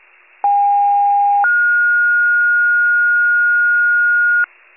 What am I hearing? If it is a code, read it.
Two-tone page, heard as a one-second tone, then a three-second tone, about 810 then 1430 Hz, with noise. It is heard from 0.4 to 4.5 s.
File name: wcalert